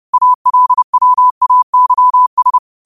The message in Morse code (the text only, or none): ALWAYS